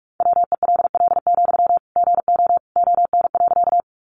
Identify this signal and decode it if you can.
Morse code, heard as a keyed tone, about 44 wpm, 700 Hz. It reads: WELL, GO ON.
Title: Sentences from Top 200 Words 45wpm